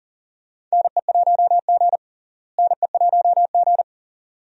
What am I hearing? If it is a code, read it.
Morse code, heard as a keyed tone, about 40 wpm, 700 Hz. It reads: NE1G NE1G
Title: USA Call Signs - 4 Characters - Repeated 2x - 40wpm